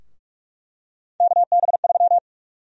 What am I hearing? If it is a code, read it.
Morse code, heard as a keyed tone, about 46 wpm, 700 Hz. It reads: KB3
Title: USA Call Sign Prefixes - 45wpm